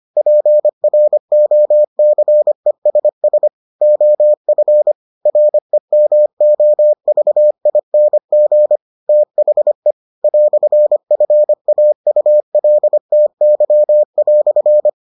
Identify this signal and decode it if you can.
Morse code, heard as a keyed tone, about 25 wpm, 600 Hz. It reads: PROCESS OF REMOVING THE "FAULTY"